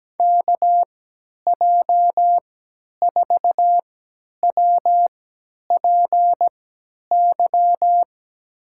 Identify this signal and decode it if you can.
Morse code, heard as a keyed tone, about 17 wpm, spaced as for 10 wpm, 700 Hz. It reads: KJ4WPY
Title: USA Call Signs - 6 Characters - 3x Character Spacing - 17wpm